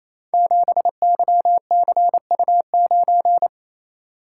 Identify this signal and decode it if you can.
Morse code, heard as a keyed tone, about 29 wpm, 700 Hz. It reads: 7YCU9
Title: Code Groups - 5 Letter:Numbers - 28wpm